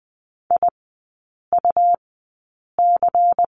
Morse code, heard as a keyed tone, about 20 wpm, spaced as for 8 wpm, 700 Hz. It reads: IUC
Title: 172 Lesson 28e - Number 9 - Three Characters 20wpm